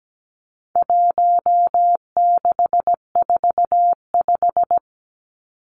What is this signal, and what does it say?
Morse code, heard as a keyed tone, about 17 wpm, 700 Hz. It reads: 1645
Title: Numbers - 4-Digit 17wpm